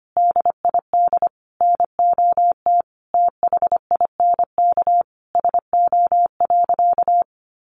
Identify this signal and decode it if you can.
Morse code, heard as a keyed tone, about 25 wpm, 700 Hz. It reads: DID NOT THINK SO.
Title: Sentences from Top 1500 Words 25wpm